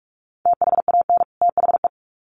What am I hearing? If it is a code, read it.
Morse code, heard as a keyed tone, about 44 wpm, 700 Hz. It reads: THAN THE
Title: Sets of 2 Words 45wpm